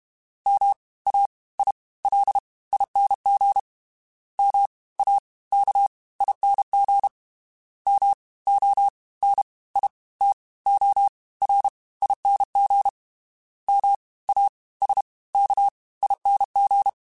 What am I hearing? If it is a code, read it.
Morse code, heard as a keyed tone, about 32 wpm, 800 Hz. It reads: M A I L ING M A K ING M O N I T O R ING M A S K ING
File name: Special INGs / Special INGs-mp3s / Fast ing @30